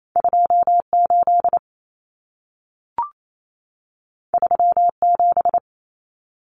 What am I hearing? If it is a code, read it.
Morse code, heard as a keyed tone, about 28 wpm, 700 Hz. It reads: 28 37
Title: Numbers - 2-Digit 28wpm